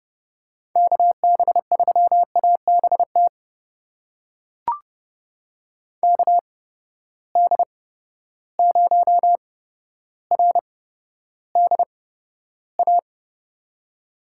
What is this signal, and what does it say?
Morse code, heard as a keyed tone, about 30 wpm, spaced as for 13 wpm, 700 Hz. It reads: KB3ABT KD0RDA
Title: USA Call Signs - 6 Characters - 8x Character Spacing - 30wpm